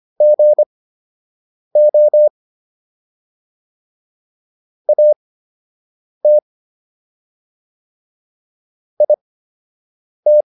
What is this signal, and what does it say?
Morse code, heard as a keyed tone, about 25 wpm, spaced as for 7 wpm, 600 Hz. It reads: GO AT IT